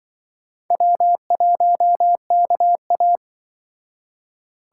Morse code, heard as a keyed tone, about 24 wpm, 700 Hz. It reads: W1KA
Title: Speed Racing - USA Call Signs 20wpm